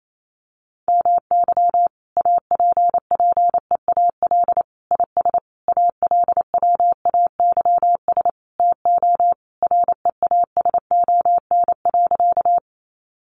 Morse code, heard as a keyed tone, about 28 wpm, 700 Hz. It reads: MY APPEAL IS ALWAYS TO REASON.
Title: Sentences from Top 1300 Words 28wpm